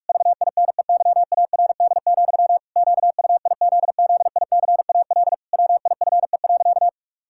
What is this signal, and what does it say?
Morse code, heard as a keyed tone, about 41 wpm, 700 Hz. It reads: VINEYARD, QUIZZICAL WIFE.
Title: Pangram Challenge v2 - 45wpm